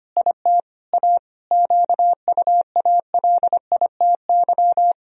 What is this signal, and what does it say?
Morse code, heard as a keyed tone, about 25 wpm, 700 Hz. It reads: IT A QUALITY
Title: Sentences from Top 1000 Words - Review Part 4 - 25wpm